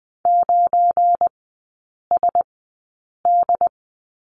Morse code, heard as a keyed tone, about 20 wpm, spaced as for 8 wpm, 700 Hz. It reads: 9SD
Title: 172 Lesson 28e - Number 9 - Three Characters 20wpm